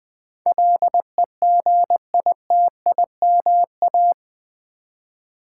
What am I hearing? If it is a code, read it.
Morse code, heard as a keyed tone, about 20 wpm, 700 Hz. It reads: LEGITIMA
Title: Build A Word - 4 Letters At A Time - 10-Letter Words - 20wpm